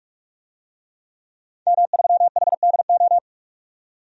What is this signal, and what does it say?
Morse code, heard as a keyed tone, about 48 wpm, 700 Hz. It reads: M3HDO
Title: DXCC-100 Call Signs - 5 Characters - 45wpm